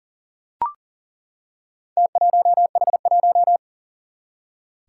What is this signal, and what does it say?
Morse code, heard as a keyed tone, about 40 wpm, 700 Hz. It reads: T1H1
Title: Four Challenging Letter-Number Combinations 40wpm